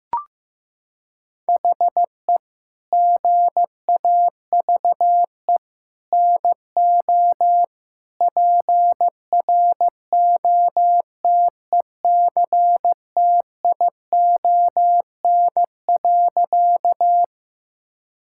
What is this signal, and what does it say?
Morse code, heard as a keyed tone, about 15 wpm, 700 Hz. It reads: HE GAVE NO PROTECTION.
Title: Sentences from Top 1600 Words 15wpm